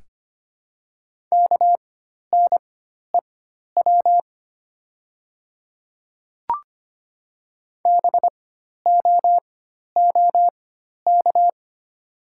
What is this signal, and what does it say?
Morse code, heard as a keyed tone, about 25 wpm, spaced as for 12 wpm, 700 Hz. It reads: KNEW BOOK
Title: 4x Character Spacing - Top 500 Words - 25wpm